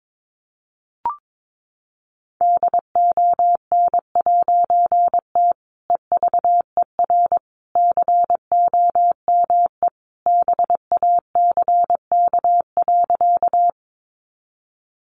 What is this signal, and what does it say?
Morse code, heard as a keyed tone, about 22 wpm, 700 Hz. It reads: DON'T EVER COME BACK.
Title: A Sound Of Thunder 22wpm